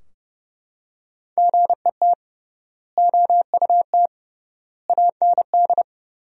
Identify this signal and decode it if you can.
Morse code, heard as a keyed tone, about 32 wpm, 700 Hz. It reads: GET OUT AND
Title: 3x Word-Spacing - Sentences from Top 200 Words - 30wpm